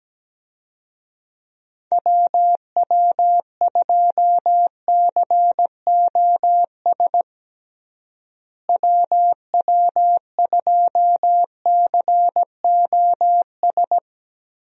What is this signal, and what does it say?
Morse code, heard as a keyed tone, about 17 wpm, 700 Hz. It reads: WW2COS WW2COS
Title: USA Call Signs - 6 Characters - Repeated 3x - 17wpm